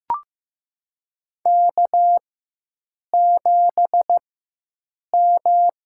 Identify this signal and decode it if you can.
Morse code, heard as a keyed tone, about 15 wpm, spaced as for 7 wpm, 700 Hz. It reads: K7M
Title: USA Call Signs - 5 Characters - 4x Character Spacing - 15wpm